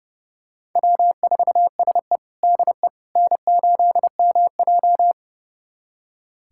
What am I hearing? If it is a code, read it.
Morse code, heard as a keyed tone, about 32 wpm, 700 Hz. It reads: W4SE DE N8MJ
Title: USA Callsign DE Callsign - 4 Characters - 30wpm